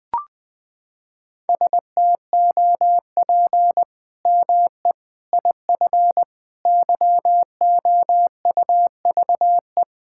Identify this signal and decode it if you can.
Morse code, heard as a keyed tone, about 19 wpm, 700 Hz. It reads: STOP ME IF YOUVE
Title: Sets of 8 Words - 20wpm